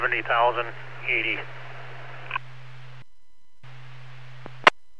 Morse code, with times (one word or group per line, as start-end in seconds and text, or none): none